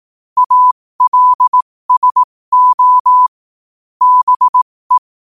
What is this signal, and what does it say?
Morse code, heard as a keyed tone, about 18 wpm, spaced as for 15 wpm, 1000 Hz. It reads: ALSO BE